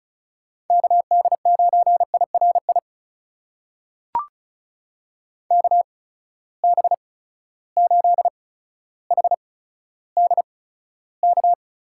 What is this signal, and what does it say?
Morse code, heard as a keyed tone, about 36 wpm, spaced as for 15 wpm, 700 Hz. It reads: KD9IRI KB8HDK
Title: USA Call Signs - 6 Characters - 8x Character Spacing - 35wpm